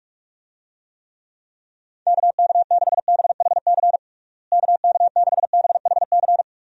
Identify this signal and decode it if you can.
Morse code, heard as a keyed tone, about 45 wpm, 700 Hz. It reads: KK6BHC KK6BHC
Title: USA Call Signs - 6 Characters - Repeated 2x - 45wpm